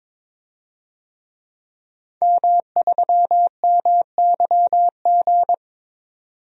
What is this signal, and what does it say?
Morse code, heard as a keyed tone, about 22 wpm, 700 Hz. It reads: M3MYG
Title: DXCC-100 Call Signs - 5 Characters - 22wpm